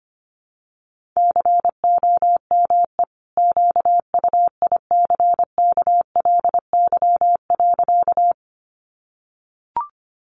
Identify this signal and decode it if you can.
Morse code, heard as a keyed tone, about 25 wpm, 700 Hz. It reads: COME QUICKLY.
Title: Sentences from Top 700 Words 25wpm